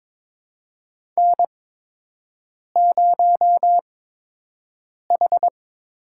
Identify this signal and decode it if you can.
Morse code, heard as a keyed tone, about 23 wpm, spaced as for 6 wpm, 700 Hz. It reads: N0H
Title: USA Call Signs - 4 Characters - 8x Character Spacing - 22wpm